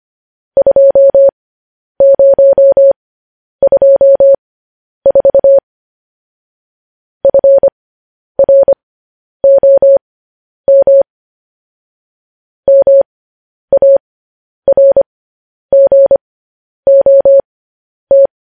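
Morse code, heard as a keyed tone, about 25 wpm, spaced as for 10 wpm, 550 Hz. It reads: 2024 FROM MARGOT
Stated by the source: News headlines sent in Morse code at 10 wpm.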